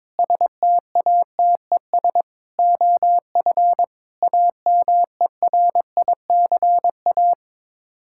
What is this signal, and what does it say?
Morse code, heard as a keyed tone, about 22 wpm, 700 Hz. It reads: STATES OF AMERICA
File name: practice / 22wpm / Sets of 8 Words - 22wpm